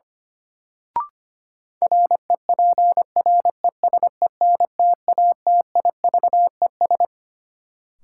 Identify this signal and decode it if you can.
Morse code, heard as a keyed tone, about 25 wpm, 700 Hz. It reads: REPRESENTATIVES
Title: Sets of 5 Words Encore - New Word Drill - 25wpm